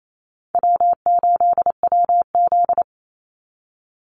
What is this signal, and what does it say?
Morse code, heard as a keyed tone, about 28 wpm, 700 Hz. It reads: W8WZ
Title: Call Signs - 28wpm